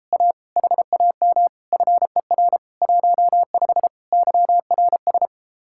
Morse code, heard as a keyed tone, about 33 wpm, 700 Hz. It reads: A HAM FER 15 YRS